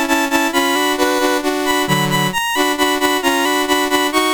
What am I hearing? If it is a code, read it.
There is no code.